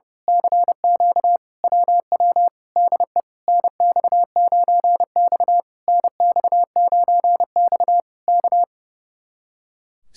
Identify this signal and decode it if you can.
Morse code, heard as a keyed tone, about 30 wpm, 700 Hz. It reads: CQ WW DE NX9X NX9X K
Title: Speed-Racing Common QSO Phrases - 30wpm